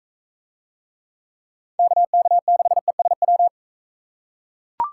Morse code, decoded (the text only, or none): KK6ESW